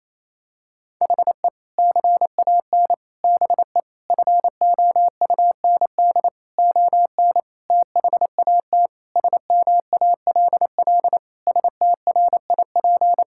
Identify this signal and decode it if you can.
Morse code, heard as a keyed tone, about 28 wpm, 700 Hz. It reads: HE CAN BE FOUND ON THAT SMALL STRIP
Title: Sentences from Top 1600 Words 28wpm